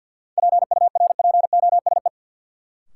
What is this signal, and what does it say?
Morse code, heard as a keyed tone, about 48 wpm, 700 Hz. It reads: PURPOSE
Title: Sets of 5 Words - New Word Drill - 50wpm